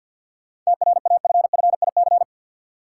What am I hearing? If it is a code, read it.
Morse code, heard as a keyed tone, about 52 wpm, 700 Hz. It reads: TRAFFIC
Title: Sentences from Top 1300 Words - New Word Drill Alt - 50wpm